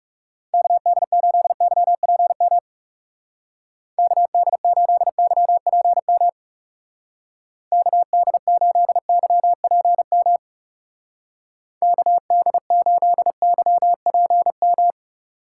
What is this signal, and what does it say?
Morse code, heard as a keyed tone, about 35 wpm, 700 Hz. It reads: KD8YPM KD8YPM KD8YPM KD8YPM
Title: Speed Racing - USA 2x3 Call Signs 30wpm